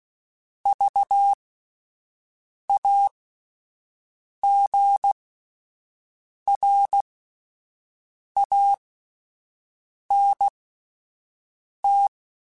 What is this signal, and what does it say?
Morse code, heard as a keyed tone, about 16 wpm, spaced as for 5 wpm, 800 Hz. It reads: VAGRANT
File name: Alphabetical / Alphabetical mp3s / Vs @5